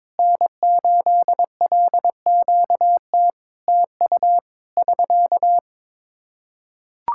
Morse code, heard as a keyed tone, about 22 wpm, 700 Hz. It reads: N8LQT TU <SK>
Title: Practice QSO - Exchange 4 - Ending - 22wpm